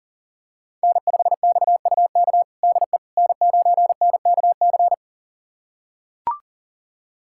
Morse code, heard as a keyed tone, about 37 wpm, 700 Hz. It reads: N5XUK DE N9NKC
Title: USA Callsign DE Callsign - 5 Characters - 40wpm